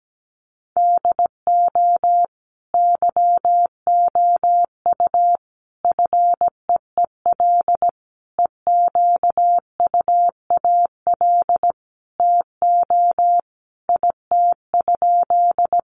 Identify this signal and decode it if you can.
Morse code, heard as a keyed tone, about 17 wpm, 700 Hz. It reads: DO YOU FEEL EQUAL TO IT?
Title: Sentences from Top 1400 Words 17wpm